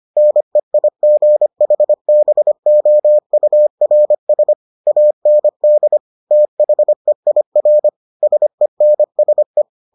Morse code, heard as a keyed tone, about 25 wpm, 600 Hz. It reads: NEIGHBOURS AND THEIR SENSE